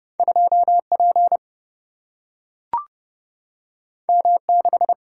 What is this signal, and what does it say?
Morse code, heard as a keyed tone, about 29 wpm, 700 Hz. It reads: 2 P M 6
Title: Single-Letter Single-Number 30wpm